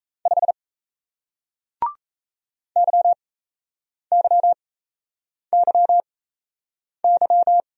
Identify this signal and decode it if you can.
Morse code, heard as a keyed tone, about 38 wpm, spaced as for 8 wpm, 700 Hz. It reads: 5 YYYY